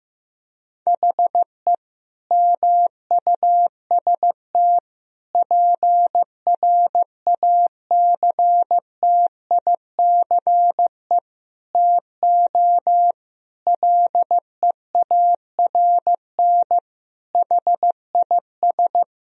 Morse code, heard as a keyed tone, about 15 wpm, 700 Hz. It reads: HE MUST PRACTICE TO LEARN HIS
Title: Sentences from Top 1000 Words - Review Part 4 - 15wpm